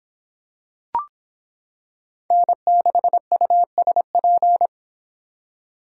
Morse code, heard as a keyed tone, about 26 wpm, 700 Hz. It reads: N6USP